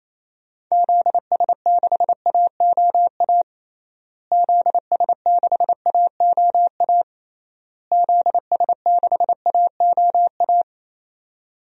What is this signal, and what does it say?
Morse code, heard as a keyed tone, about 28 wpm, 700 Hz. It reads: ZS6AOA ZS6AOA ZS6AOA
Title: DXCC-100 Call Signs - 6 Characters - Repeated 3x - 28wpm